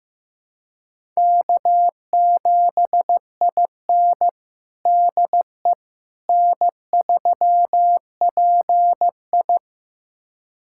Morse code, heard as a keyed tone, about 15 wpm, 700 Hz. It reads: K7IN DE N3PI
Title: USA Callsign DE Callsign - 4 Characters - 15wpm